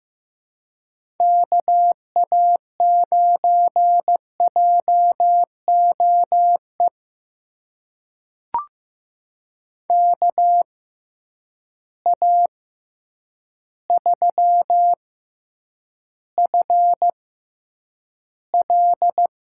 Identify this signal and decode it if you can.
Morse code, heard as a keyed tone, about 15 wpm, spaced as for 8 wpm, 700 Hz. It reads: KA9JOE KA3FL
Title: USA Call Signs - 6 Characters - 6x Character Spacing - 15wpm